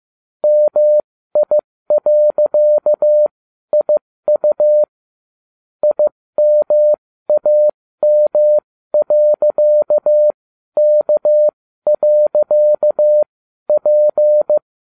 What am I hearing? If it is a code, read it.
Morse code, heard as a keyed tone, about 15 wpm, spaced as for 12 wpm, 600 Hz. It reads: MI.IU IMAM.K.P
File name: cw / ubacw12 2